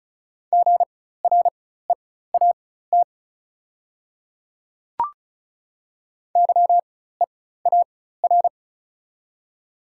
Morse code, heard as a keyed tone, about 36 wpm, spaced as for 16 wpm, 690 Hz. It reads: GREAT YEAR
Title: 4x Character Spacing - Top 300 Words - 35wpm